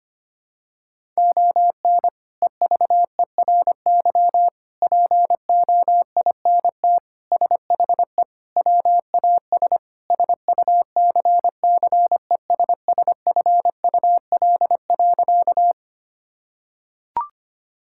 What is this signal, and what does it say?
Morse code, heard as a keyed tone, about 25 wpm, 700 Hz. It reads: ON EVERY POINT SHE WAS SUCCESSFUL.
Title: Sentences from Top 1000 Words - Review Part 6 - 25wpm